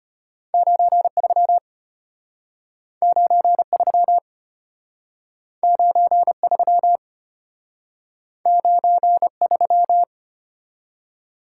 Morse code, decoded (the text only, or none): 93 93 93 93